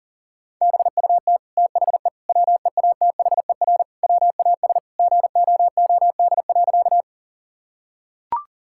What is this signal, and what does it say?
Morse code, heard as a keyed tone, about 40 wpm, 700 Hz. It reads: BUT THE WEATHER WAS GOOD.
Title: Sentences from Top 1000 Words - Review Part 5 - 40wpm